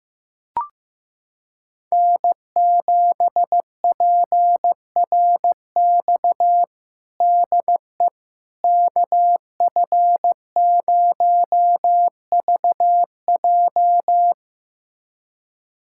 Morse code, heard as a keyed tone, about 15 wpm, 700 Hz. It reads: N7PRX DE KF0VJ
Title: USA Callsign DE Callsign - 5 Characters - 15wpm